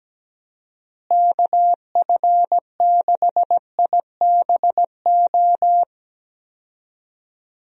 Morse code, heard as a keyed tone, about 17 wpm, 700 Hz. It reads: KF6IBO
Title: USA Call Signs - 6 Characters - Repeated 2x - 17wpm